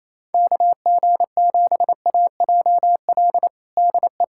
Morse code, heard as a keyed tone, about 28 wpm, 700 Hz. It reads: KG7AJL DE
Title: USA Callsign DE Callsign - 6 Characters - 28wpm